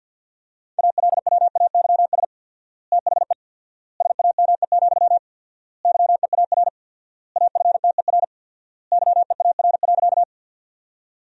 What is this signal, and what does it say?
Morse code, heard as a keyed tone, about 50 wpm, 700 Hz. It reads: ALWAYS THE SAME, YEAR AFTER YEAR.
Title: 4x Word-Spacing - Sentences from Top 300 Words - 50wpm